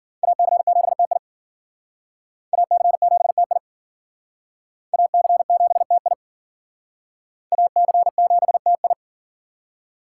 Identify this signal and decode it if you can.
Morse code, heard as a keyed tone, about 53 wpm, 700 Hz. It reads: AC7TI AC7TI AC7TI AC7TI